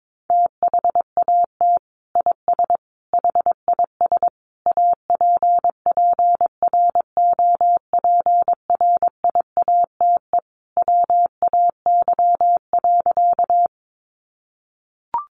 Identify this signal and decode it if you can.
Morse code, heard as a keyed tone, about 22 wpm, 700 Hz. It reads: THAT IS HIS APPROPRIATE WAY.
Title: Sentences from Top 900 Words 22wpm